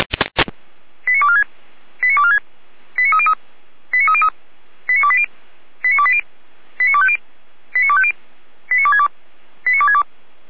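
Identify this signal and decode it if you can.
Five-tone selective call (ZVEI), heard as quick runs of five tones, tones of about 70 ms. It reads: 89246 89246 89303 89303 89280 89280 89250 89250 89262 89262
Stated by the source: ZVEI five-tone selective call received by SDR, from the Feuerwehr Orlamünde audio archive.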